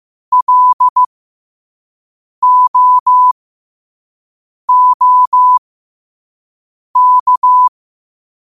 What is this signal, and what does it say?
Morse code, heard as a keyed tone, about 15 wpm, spaced as for 5 wpm, 1000 Hz. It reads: LOOK
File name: qotdfiles / 5wpm